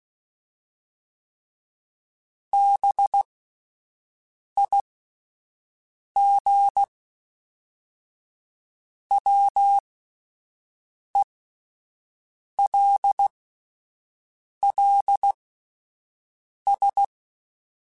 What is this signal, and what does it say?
Morse code, heard as a keyed tone, about 16 wpm, spaced as for 5 wpm, 800 Hz. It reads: BIGWELLS
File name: fromSMs / FromSMs mp3s / Cities@5